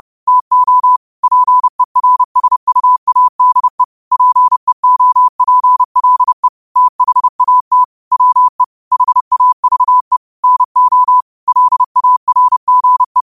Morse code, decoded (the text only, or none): TO PERSUADE PEOPLE THAT WE HAVE NO LARGE